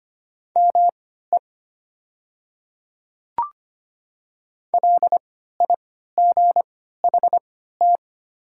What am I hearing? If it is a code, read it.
Morse code, heard as a keyed tone, about 25 wpm, spaced as for 14 wpm, 700 Hz. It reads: ME LIGHT
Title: 3x Character Spacing - Top 200 Words - 25wpm